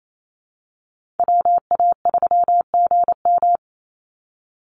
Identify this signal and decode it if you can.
Morse code, heard as a keyed tone, about 28 wpm, 700 Hz. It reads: WA3GM
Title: Call Signs Repeated 2x - 28wpm